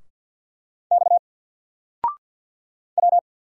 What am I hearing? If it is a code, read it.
Morse code, heard as a keyed tone, about 48 wpm, 700 Hz. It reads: X W
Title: Single Character (Letter:Number:Punctuation:Prosign) Alt 50wpm